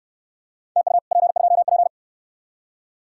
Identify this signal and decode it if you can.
Morse code, heard as a keyed tone, about 60 wpm, 700 Hz. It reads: TU P2P